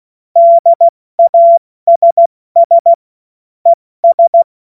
Morse code, heard as a keyed tone, about 16 wpm, 680 Hz. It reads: DASS ES
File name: Morsekurs / Witze / Witz 26 März 2024 70 F80